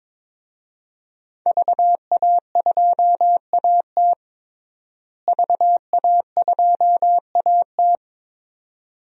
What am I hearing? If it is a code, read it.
Morse code, heard as a keyed tone, about 22 wpm, 700 Hz. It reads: VA2AT VA2AT
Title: DXCC-100 Call Signs - 5 Characters - Repeated 2x - 22wpm